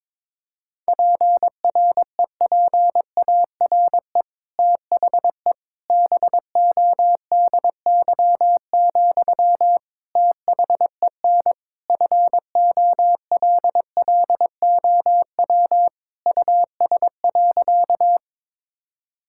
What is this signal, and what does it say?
Morse code, heard as a keyed tone, about 22 wpm, 700 Hz. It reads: PREPARE THE BODY, THEN FOLLOW US.
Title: Sentences from Top 1000 Words - Review Part 6 - 22wpm